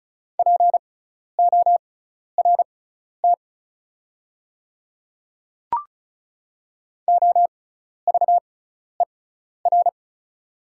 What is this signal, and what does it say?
Morse code, heard as a keyed tone, about 35 wpm, spaced as for 12 wpm, 690 Hz. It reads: PORT OVER
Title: 6x Character Spacing - Top 200 Words - 35wpm